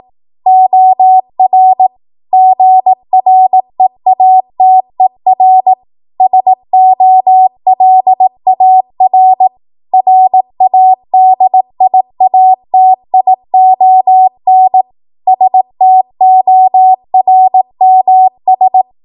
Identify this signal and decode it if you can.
Morse code, heard as a keyed tone, about 18 wpm, 750 Hz. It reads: OR GREATER SOLAR RADIATION STORMS
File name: Morse / 18WPM